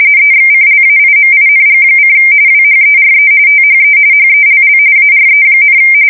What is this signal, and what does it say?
RTTY, about 45.45 baud. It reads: EXCLAIMING: WHERE THE HELL ARE,
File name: fsk170 / battleshipnj